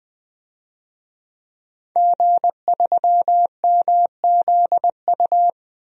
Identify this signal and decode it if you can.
Morse code, heard as a keyed tone, about 20 wpm, 700 Hz. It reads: G3MZU
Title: DXCC-100 Call Signs - 5 Characters - 20wpm